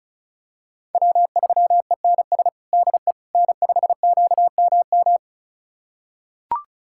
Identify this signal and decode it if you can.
Morse code, heard as a keyed tone, about 34 wpm, 700 Hz. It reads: W3ENS DE N5QMM
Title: USA Callsign DE Callsign - 5 Characters - 35wpm